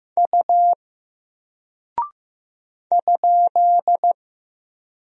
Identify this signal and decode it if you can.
Morse code, heard as a keyed tone, about 15 wpm, 700 Hz. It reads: U ?
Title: Single Character (Letter:Number:Punctuation:Prosign) Alt 15wpm